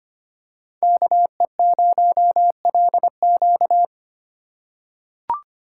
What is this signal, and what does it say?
Morse code, heard as a keyed tone, about 25 wpm, 700 Hz. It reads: KE0LQ
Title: USA Call Signs - 5 Characters - 3x Character Spacing - 25wpm